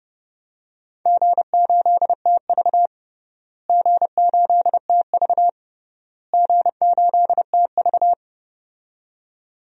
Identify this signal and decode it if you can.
Morse code, heard as a keyed tone, about 30 wpm, 700 Hz. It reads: G8TV G8TV G8TV
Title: DXCC-100 Call Signs - 4 Characters - Repeated 3x - 30wpm